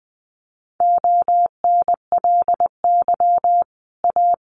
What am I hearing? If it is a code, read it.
Morse code, heard as a keyed tone, about 20 wpm, 700 Hz. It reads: ONLY A
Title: Sets of 5 Words Encore 20wpm